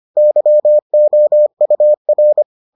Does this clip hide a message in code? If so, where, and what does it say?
Yes: Morse code from 0.2 to 2.4 s, heard as a keyed tone, about 25 wpm, 600 Hz. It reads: YOUR